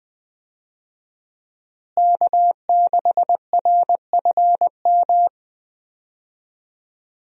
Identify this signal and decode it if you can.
Morse code, heard as a keyed tone, about 20 wpm, 700 Hz. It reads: K6RFM
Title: USA Call Signs - 5 Characters - Repeated 2x - 20wpm